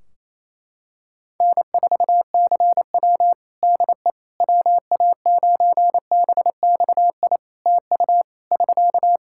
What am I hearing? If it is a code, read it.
Morse code, heard as a keyed tone, about 27 wpm, 700 Hz. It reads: N4CW DE WA9BXI TU <SK>
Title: Practice QSO - Exchange 4 - Ending - 28wpm